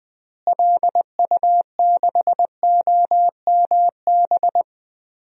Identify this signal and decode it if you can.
Morse code, heard as a keyed tone, about 19 wpm, 700 Hz. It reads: LU6OMB
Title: DXCC-100 Call Signs - 6 Characters - Repeated 3x - 20wpm